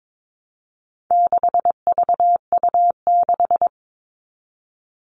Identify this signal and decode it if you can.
Morse code, heard as a keyed tone, about 22 wpm, 700 Hz. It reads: 6VU6
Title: All Cut Numbers - 4-Digit 22wpm